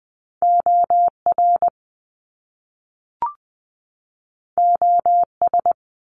Morse code, heard as a keyed tone, about 20 wpm, 700 Hz. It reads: OR OS
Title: Top Two Letter Combinations in Words 20wpm